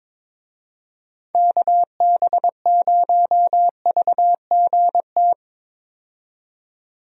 Morse code, heard as a keyed tone, about 22 wpm, 700 Hz. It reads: KB0VGT
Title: USA Call Signs - 6 Characters - 22wpm